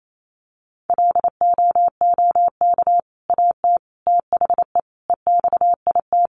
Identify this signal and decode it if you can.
Morse code, heard as a keyed tone, about 28 wpm, 700 Hz. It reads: LOOK AT THE EXIT
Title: Sentences from Top 1100 Words 28wpm